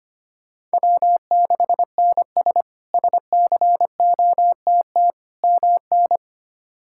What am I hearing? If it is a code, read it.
Morse code, heard as a keyed tone, about 25 wpm, 700 Hz. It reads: W6NS SCOTT MN
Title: K1USN SST QSO Practice - 25wpm